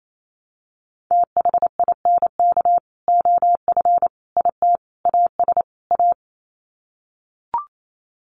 Morse code, heard as a keyed tone, about 28 wpm, 700 Hz. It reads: THINK OF IT AS A
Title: Sets of 5 Words Encore2 28wpm